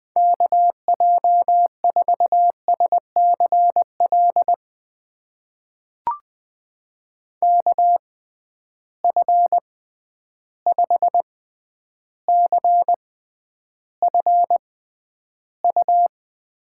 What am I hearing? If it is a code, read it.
Morse code, heard as a keyed tone, about 21 wpm, spaced as for 10 wpm, 700 Hz. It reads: KJ4SCL KF5CFU
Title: USA Call Signs - 6 Characters - 6x Character Spacing - 20wpm